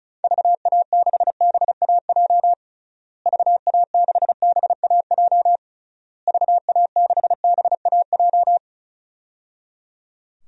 Morse code, heard as a keyed tone, about 34 wpm, 700 Hz. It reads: VA6BAJ VA6BAJ VA6BAJ
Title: DXCC-100 Call Signs - 6 Characters - Repeated 3x - 35wpm